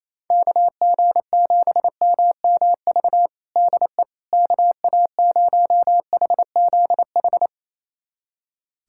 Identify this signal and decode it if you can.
Morse code, heard as a keyed tone, about 28 wpm, 700 Hz. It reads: KG7MMV DE KA0HZH
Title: USA Callsign DE Callsign 28wpm